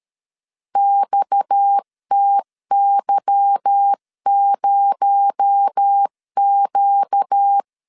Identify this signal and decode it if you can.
Morse code, heard as a keyed tone, about 13 wpm, 800 Hz. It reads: XTY0Q
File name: LEZIONI / 9 286 12wpm 2305